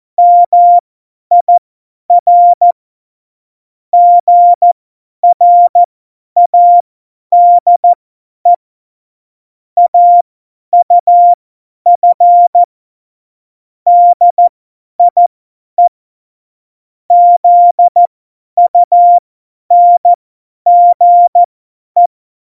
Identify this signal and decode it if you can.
Morse code, heard as a keyed tone, about 14 wpm, spaced as for 10 wpm, 700 Hz. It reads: MIR GRADE AUF DIE ZUNGE
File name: Morsekurs / Witze / Witz 17 April 2018, 50 F70